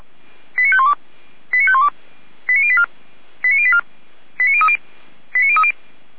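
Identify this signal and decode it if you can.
Five-tone selective call (ZVEI), heard as quick runs of five tones, tones of about 70 ms. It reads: 89512 89512 89084 89084 89030 89030
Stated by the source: ZVEI five-tone selective call received on an SDR.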